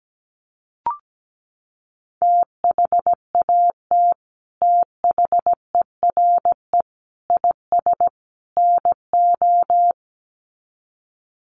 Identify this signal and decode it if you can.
Morse code, heard as a keyed tone, about 17 wpm, 700 Hz. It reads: THAT THERE IS NO
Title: Sets of 4 Words 17wpm